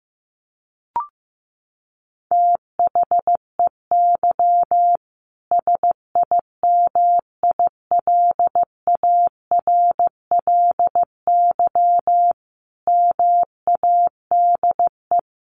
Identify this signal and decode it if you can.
Morse code, heard as a keyed tone, about 15 wpm, 700 Hz. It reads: THEY SIMILARLY MADE
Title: Sentences from Top 1400 Words 15wpm